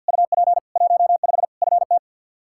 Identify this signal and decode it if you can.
Morse code, heard as a keyed tone, about 49 wpm, 700 Hz. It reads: UP 15 FT
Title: Practice QSO - Exchange 2 - Info - 50wpm